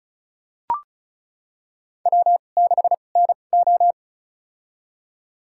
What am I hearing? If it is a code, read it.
Morse code, heard as a keyed tone, about 36 wpm, spaced as for 25 wpm, 700 Hz. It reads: W6NO
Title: USA Call Signs - 4 Characters - 2x Character Spacing - 35wpm